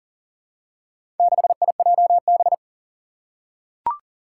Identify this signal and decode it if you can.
Morse code, heard as a keyed tone, about 37 wpm, 700 Hz. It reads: 6IJB
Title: Four Challenging Letter-Number Combinations 40wpm